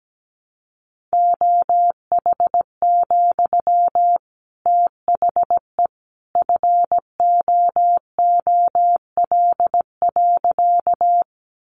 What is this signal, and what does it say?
Morse code, heard as a keyed tone, about 17 wpm, 700 Hz. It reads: OH, THE FOOL.
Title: A Sound Of Thunder 17wpm